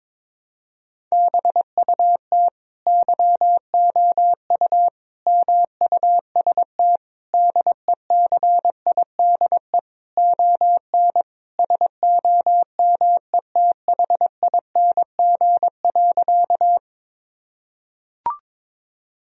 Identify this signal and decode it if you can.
Morse code, heard as a keyed tone, about 22 wpm, 700 Hz. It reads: BUT YOU MUST DECIDE ON SOMETHING.
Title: Sentences from Top 1000 Words - Review Part 3 - 22wpm